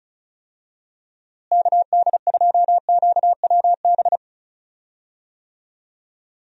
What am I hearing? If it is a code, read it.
Morse code, heard as a keyed tone, about 34 wpm, 700 Hz. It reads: KD2QWB
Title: USA Call Signs - 6 Characters - Repeated 3x - 35wpm